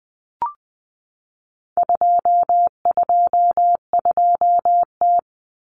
Morse code, heard as a keyed tone, about 20 wpm, 700 Hz. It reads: 222T
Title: Common Cut Numbers - 4-Digit 20wpm